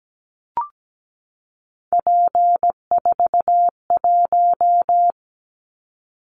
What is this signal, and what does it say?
Morse code, heard as a keyed tone, about 17 wpm, 700 Hz. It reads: P41
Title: Callsign Prefixes for Top 100 DXCC - 17wpm